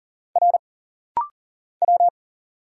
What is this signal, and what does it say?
Morse code, heard as a keyed tone, about 40 wpm, 700 Hz. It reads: R W
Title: Single Letter-Number Alt - Rapid-Fire 40wpm